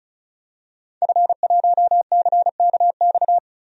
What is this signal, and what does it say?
Morse code, heard as a keyed tone, about 34 wpm, 700 Hz. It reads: F1CKX
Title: DXCC-100 Call Signs - 5 Characters - Repeated 2x - 35wpm